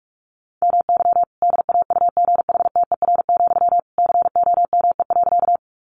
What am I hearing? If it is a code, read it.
Morse code, heard as a keyed tone, about 46 wpm, 700 Hz. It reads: MY DAUGHTER, COME.
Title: Sentences from Top 1500 Words 45wpm